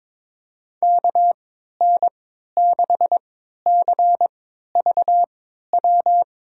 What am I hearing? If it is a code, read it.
Morse code, heard as a keyed tone, about 22 wpm, spaced as for 12 wpm, 700 Hz. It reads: KN6CVW
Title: USA Call Signs - 6 Characters - 3x Character Spacing - 22wpm